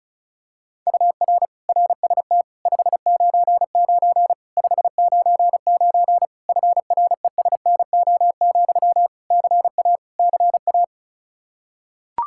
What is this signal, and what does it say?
Morse code, heard as a keyed tone, about 36 wpm, 700 Hz. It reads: UR RST 599 599 FRESNO, CA CA
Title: Practice QSO - Exchange 1 - Beginning - 35wpm